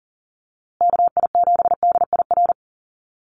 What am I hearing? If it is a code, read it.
Morse code, heard as a keyed tone, about 40 wpm, 700 Hz. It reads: KI7DIR
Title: USA Call Signs Repeated 3x 40wpm